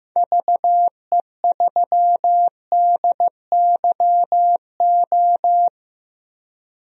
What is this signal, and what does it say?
Morse code, heard as a keyed tone, about 15 wpm, 700 Hz. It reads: VE3DYO
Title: DXCC-100 Call Signs - 6 Characters - Repeated 3x - 15wpm